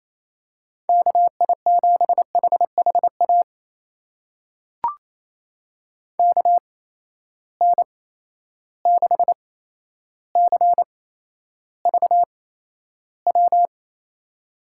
Morse code, heard as a keyed tone, about 28 wpm, spaced as for 12 wpm, 700 Hz. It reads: KI7HHA KN6CVW
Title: USA Call Signs - 6 Characters - 8x Character Spacing - 28wpm